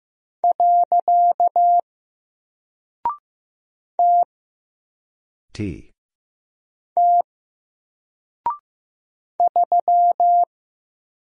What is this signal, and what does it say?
Morse code, heard as a keyed tone, about 15 wpm, 700 Hz. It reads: . T T 3
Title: Single Character (Letter:Number:Punctuation:Prosign) Alt 15wpm